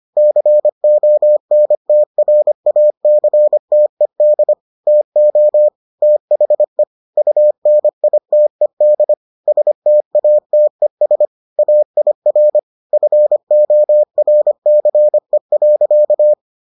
Morse code, heard as a keyed tone, about 25 wpm, 600 Hz. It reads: CONTRACTED TO THE UNITED STATES AIR FORCE.